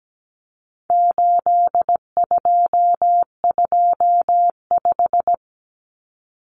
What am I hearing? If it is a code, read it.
Morse code, heard as a keyed tone, about 17 wpm, 700 Hz. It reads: 8225
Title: Numbers - 4-Digit 17wpm